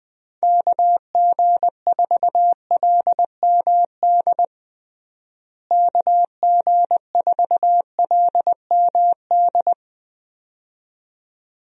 Morse code, heard as a keyed tone, about 20 wpm, 700 Hz. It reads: KG4LMD KG4LMD
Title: USA Call Signs - 6 Characters - Repeated 2x - 20wpm